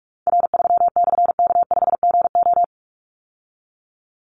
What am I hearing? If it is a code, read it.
Morse code, heard as a keyed tone, about 44 wpm, 700 Hz. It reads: R3/K5GO
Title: Call Signs Repeated 3x - 45wpm